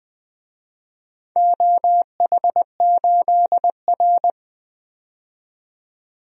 Morse code, heard as a keyed tone, about 20 wpm, 700 Hz. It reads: OH8R
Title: DXCC-100 Call Signs - 4 Characters - 20wpm